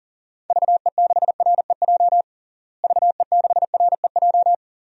Morse code, heard as a keyed tone, about 40 wpm, 700 Hz. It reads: VE6REJ VE6REJ
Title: DXCC-100 Call Signs - 6 Characters - Repeated 3x - 40wpm